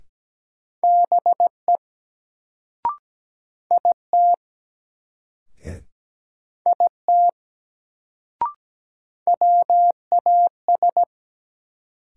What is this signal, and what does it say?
Morse code, heard as a keyed tone, about 17 wpm, 700 Hz. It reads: BE IT IT WAS
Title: Sentences from Top 100 Words - New Word Drill - 17wpm